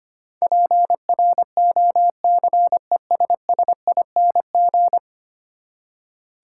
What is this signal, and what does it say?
Morse code, heard as a keyed tone, about 25 wpm, 700 Hz. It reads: PROCESSING
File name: Ten Letter Words - 25wpm